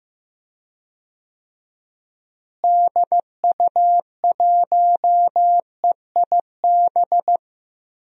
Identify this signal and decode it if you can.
Morse code, heard as a keyed tone, about 15 wpm, 700 Hz. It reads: DU1EIB
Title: DXCC-100 Call Signs - 6 Characters - Repeated 3x - 15wpm